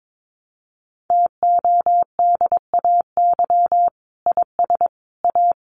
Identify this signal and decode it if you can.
Morse code, heard as a keyed tone, about 22 wpm, 700 Hz. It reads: TODAY IS A
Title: Sentences from Top 600 Words 22wpm